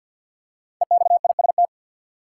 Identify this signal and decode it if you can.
Morse code, heard as a keyed tone, about 53 wpm, 700 Hz. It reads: EXIST
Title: Five Letter Words - 50wpm